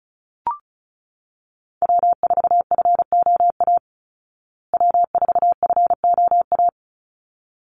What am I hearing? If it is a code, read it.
Morse code, heard as a keyed tone, about 34 wpm, 700 Hz. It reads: W4FOA W4FOA
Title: Call Signs Repeated 3x - 35wpm